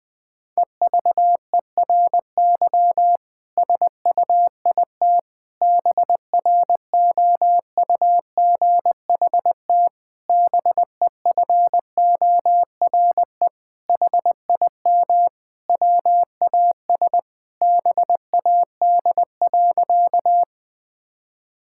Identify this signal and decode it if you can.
Morse code, heard as a keyed tone, about 20 wpm, 700 Hz. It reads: EVERY SUIT BROUGHT BEFORE HIM WAS BAD.
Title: Sentences from Top 1600 Words 20wpm